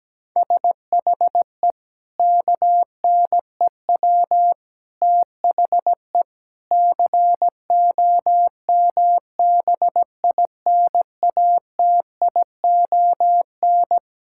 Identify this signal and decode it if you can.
Morse code, heard as a keyed tone, about 17 wpm, 700 Hz. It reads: SHE KNEW THE COMBINATION
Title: Sentences from Top 1000 Words - Review Part 6 - 17wpm